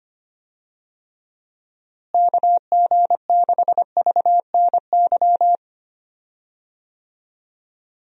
Morse code, heard as a keyed tone, about 25 wpm, 700 Hz. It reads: KG6VNY
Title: USA Call Signs - 6 Characters - Repeated 3x - 25wpm